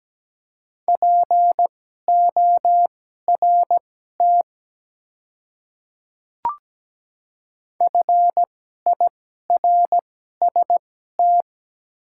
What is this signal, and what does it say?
Morse code, heard as a keyed tone, about 17 wpm, spaced as for 12 wpm, 700 Hz. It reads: PORT FIRST
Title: 2x Character Spacing - Top 200 Words - 17wpm